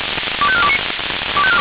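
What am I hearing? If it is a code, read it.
Five-tone selective call (ZVEI), heard as quick runs of five tones, tones of about 70 ms. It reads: 25429
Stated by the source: ZVEI five-tone selective call received on an SDR.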